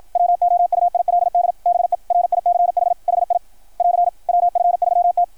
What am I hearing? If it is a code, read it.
Morse code, heard as a keyed tone, about 53 wpm, 690 Hz. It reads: WOUILD BE RICH HI <AR> WF3T
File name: archived / CRSnet